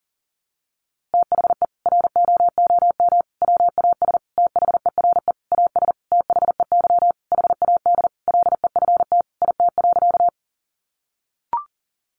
Morse code, heard as a keyed tone, about 40 wpm, 700 Hz. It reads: THE ROOM WAS THERE AS THEY HAD LEFT IT.
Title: A Sound Of Thunder 40wpm